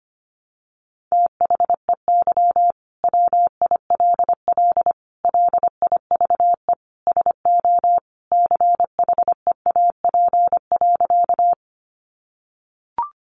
Morse code, heard as a keyed tone, about 25 wpm, 700 Hz. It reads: THEY WILL LIVE SO CHEAP.
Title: Sentences from Top 1200 Words 25wpm